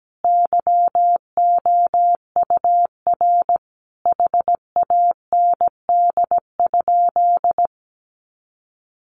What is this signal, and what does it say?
Morse code, heard as a keyed tone, about 17 wpm, 700 Hz. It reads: YOUR HAND?
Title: Sentences from Top 200 Words 17wpm